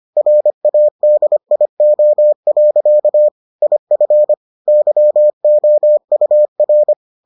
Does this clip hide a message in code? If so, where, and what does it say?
Yes: Morse code from 0.2 to 6.9 s, heard as a keyed tone, about 25 wpm, 600 Hz. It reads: RADIO. IF YOUR